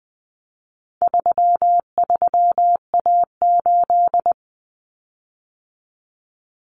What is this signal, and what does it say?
Morse code, heard as a keyed tone, about 20 wpm, 700 Hz. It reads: 33A8
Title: Common Cut Numbers - 4-Digit 20wpm